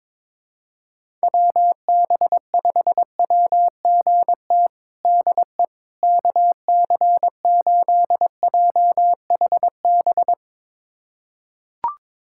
Morse code, heard as a keyed tone, about 22 wpm, 700 Hz. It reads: WB5WGT DE KC8JHB
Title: USA Callsign DE Callsign 22wpm